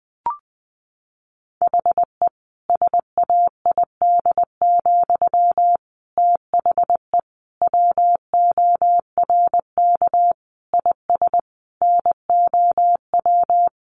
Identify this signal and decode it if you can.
Morse code, heard as a keyed tone, about 20 wpm, 700 Hz. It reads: HE SAID, THE WORK IS NOW
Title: Sentences from Top 400 Words 20wpm